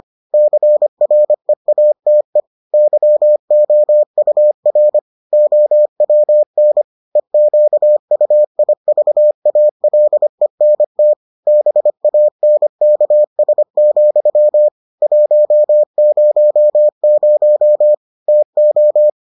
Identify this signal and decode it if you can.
Morse code, heard as a keyed tone, about 25 wpm, 600 Hz. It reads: CREATE YOUR OWN EQUIVALENT BANKS, 100 TO